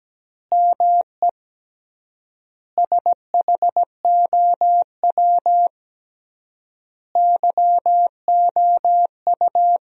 Morse code, heard as a keyed tone, about 17 wpm, 700 Hz. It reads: ME SHOW YOU
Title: 3x Word-Spacing - Sentences from Top 500 Words - 17wpm